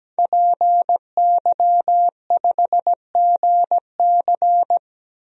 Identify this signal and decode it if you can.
Morse code, heard as a keyed tone, about 17 wpm, 700 Hz. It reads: PY5GC
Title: DXCC-100 Call Signs - 5 Characters - Repeated 2x - 17wpm